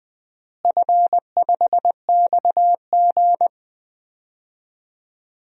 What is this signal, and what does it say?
Morse code, heard as a keyed tone, about 20 wpm, 700 Hz. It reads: F5XG
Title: DXCC-100 Call Signs - 4 Characters - Repeated 3x - 20wpm